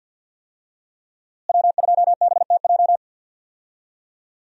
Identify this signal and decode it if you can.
Morse code, heard as a keyed tone, about 51 wpm, 700 Hz. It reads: W2BTJ
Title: USA Call Signs - 5 Characters - Repeated 2x - 50wpm